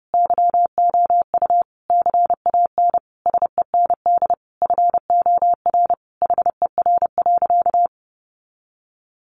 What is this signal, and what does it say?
Morse code, heard as a keyed tone, about 30 wpm, 700 Hz. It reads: YOU CAN SEND FOR HER.
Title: Sentences from Top 1500 Words 30wpm